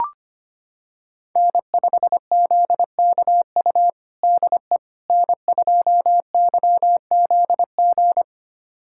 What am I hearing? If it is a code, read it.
Morse code, heard as a keyed tone, about 25 wpm, 700 Hz. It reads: N5ZKU DE N2YZG
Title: USA Callsign DE Callsign - 5 Characters - 25wpm